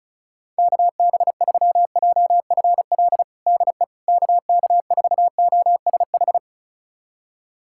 Morse code, heard as a keyed tone, about 34 wpm, 700 Hz. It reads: KB3JFL DE KK4OSH
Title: USA Callsign DE Callsign 35wpm